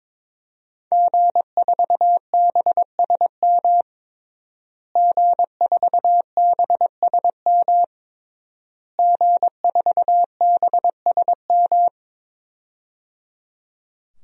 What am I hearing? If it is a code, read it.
Morse code, heard as a keyed tone, about 22 wpm, 700 Hz. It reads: G4BSM G4BSM G4BSM
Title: DXCC-100 Call Signs - 5 Characters - Repeated 3x - 22wpm